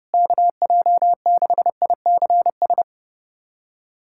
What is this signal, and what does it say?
Morse code, heard as a keyed tone, about 32 wpm, 700 Hz. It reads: KJ6ICS
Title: USA Callsign DE Callsign 30wpm